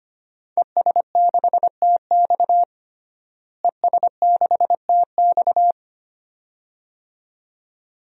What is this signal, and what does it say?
Morse code, heard as a keyed tone, about 25 wpm, 700 Hz. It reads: ES6TX ES6TX
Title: DXCC-100 Call Signs - 5 Characters - Repeated 2x - 25wpm